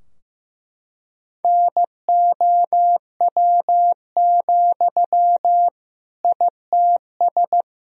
Morse code, heard as a keyed tone, about 15 wpm, 700 Hz. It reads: NOW, ITS
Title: Sentences from Top 1000 Words - Review Part 3 - 15wpm